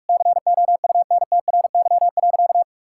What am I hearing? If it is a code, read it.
Morse code, heard as a keyed tone, about 41 wpm, 700 Hz. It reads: COUNTRY.
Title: Sentences from Top 1000 Words - Review Part 6 - 45wpm